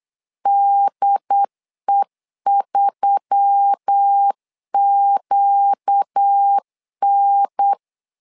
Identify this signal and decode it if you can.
Morse code, heard as a keyed tone, about 8 wpm, 800 Hz. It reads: DE3QN
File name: LEZIONI / 4 Q1S3 8wpm 3037